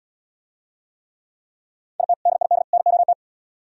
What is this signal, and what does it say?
Morse code, heard as a keyed tone, about 77 wpm, 700 Hz. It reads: IT DID NOT